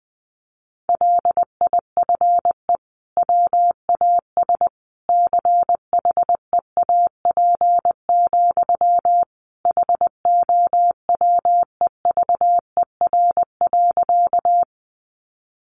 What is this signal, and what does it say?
Morse code, heard as a keyed tone, about 20 wpm, 700 Hz. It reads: LIFE WAS CHEAP, HOWEVER.
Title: Sentences from Top 1200 Words 20wpm